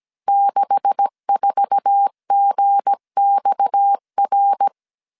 Morse code, heard as a keyed tone, about 17 wpm, 800 Hz. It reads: 64GXR
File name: CW / DET / 1 9 1500C S A 16wpm 2828